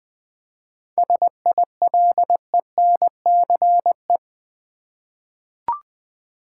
Morse code, heard as a keyed tone, about 20 wpm, 700 Hz. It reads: SILENCE
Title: 7 Letter Words - First 3 Letters Given - 20wpm